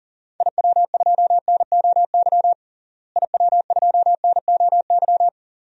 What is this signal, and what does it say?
Morse code, heard as a keyed tone, about 40 wpm, 700 Hz. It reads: IW2NOY IW2NOY
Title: DXCC-100 Call Signs - 6 Characters - Repeated 2x - 40wpm